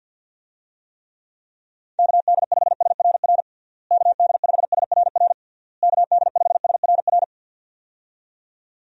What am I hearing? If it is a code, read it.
Morse code, heard as a keyed tone, about 52 wpm, 700 Hz. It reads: KD5SRR KD5SRR KD5SRR
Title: USA Call Signs - 6 Characters - Repeated 3x - 50wpm